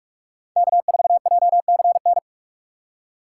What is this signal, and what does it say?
Morse code, heard as a keyed tone, about 45 wpm, 700 Hz. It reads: K4JCN
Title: USA Call Signs - 5 Characters - 8x Character Spacing - 45wpm